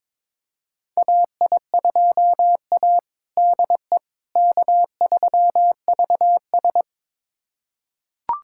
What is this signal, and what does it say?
Morse code, heard as a keyed tone, about 21 wpm, 700 Hz. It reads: AI2A DE K3VS
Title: USA Callsign DE Callsign - 4 Characters - 22wpm